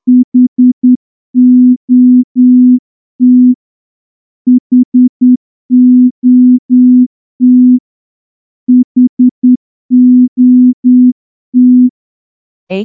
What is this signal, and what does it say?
Morse code, heard as a keyed tone, about 10 wpm, 250 Hz. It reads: HOT HOT HOT
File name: morse-training-10WPM